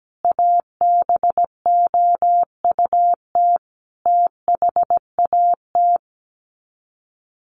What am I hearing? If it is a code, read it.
Morse code, heard as a keyed tone, about 17 wpm, 700 Hz. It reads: ABOUT THAT
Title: Sets of 5 Words Encore2 17wpm